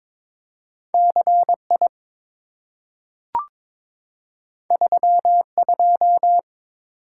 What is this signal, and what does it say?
Morse code, heard as a keyed tone, about 22 wpm, 700 Hz. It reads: CI 32
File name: practice / 22wpm / Two Letter-Number Combinations - 22wpm